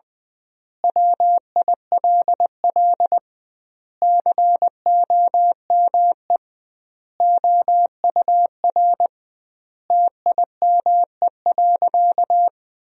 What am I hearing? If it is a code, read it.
Morse code, heard as a keyed tone, about 20 wpm, 700 Hz. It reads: WILL COME OUR TIME.
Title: 2x Word-Spacing - Sentences from Top 200 Words - 20wpm